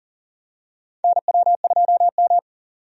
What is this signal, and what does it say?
Morse code, heard as a keyed tone, about 37 wpm, 700 Hz. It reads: NW2M
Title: USA Call Signs - 4 Characters - 40wpm